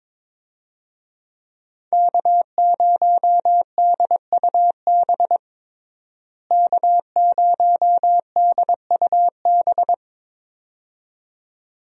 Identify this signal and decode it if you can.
Morse code, heard as a keyed tone, about 22 wpm, 700 Hz. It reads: K0DUB K0DUB
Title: USA Call Signs - 5 Characters - Repeated 2x - 22wpm